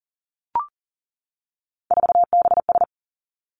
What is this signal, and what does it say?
Morse code, heard as a keyed tone, about 42 wpm, 700 Hz. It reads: 4BS
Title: Three Challenging Letter-Number Combinations 40wpm